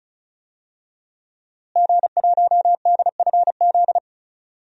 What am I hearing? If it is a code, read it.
Morse code, heard as a keyed tone, about 36 wpm, 700 Hz. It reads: G1DFZ
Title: DXCC-100 Call Signs - 5 Characters - Repeated 3x - 35wpm